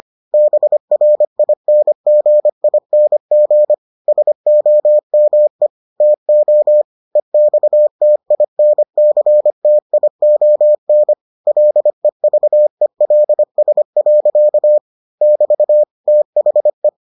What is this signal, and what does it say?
Morse code, heard as a keyed tone, about 25 wpm, 600 Hz. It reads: BRINGING SOME TO EXTINCTION LEVELS. <BT> THE